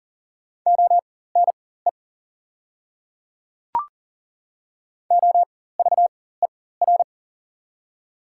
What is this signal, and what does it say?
Morse code, heard as a keyed tone, about 40 wpm, spaced as for 19 wpm, 700 Hz. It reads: ONE OVER
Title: 4x Character Spacing - Top 100 Words - 40wpm